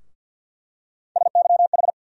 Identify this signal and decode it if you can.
Morse code, heard as a keyed tone, about 46 wpm, 700 Hz. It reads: SYH
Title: USA Call Sign Postfixes - 3 Letters 50wpm